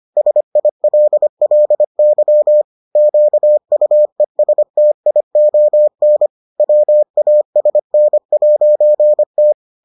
Morse code, heard as a keyed tone, about 25 wpm, 600 Hz. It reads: SILLY QUESTION WASN'T